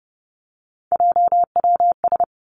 Morse code, heard as a keyed tone, about 30 wpm, 700 Hz. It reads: JWS
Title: Three Challenging Letter-Number Combinations 30wpm